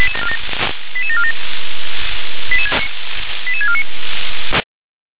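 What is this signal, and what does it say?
Five-tone selective call (ZVEI), heard as quick runs of five tones, tones of about 70 ms. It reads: 88747 88649
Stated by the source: Five-tone ZVEI selective call received on an SDR.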